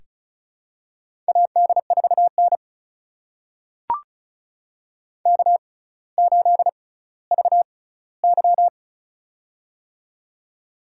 Morse code, heard as a keyed tone, about 34 wpm, spaced as for 18 wpm, 700 Hz. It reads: AD4N K8VY
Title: USA Call Signs - 4 Characters - 6x Character Spacing - 35wpm